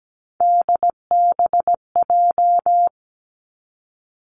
Morse code, heard as a keyed tone, about 17 wpm, 700 Hz. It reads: DBJ
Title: Three Challenging Letter-Number Combinations 17wpm